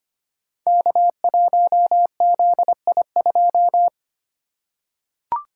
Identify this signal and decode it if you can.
Morse code, heard as a keyed tone, about 25 wpm, 700 Hz. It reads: K1ZI2
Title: Code Groups - 5 Letter:Numbers - 25wpm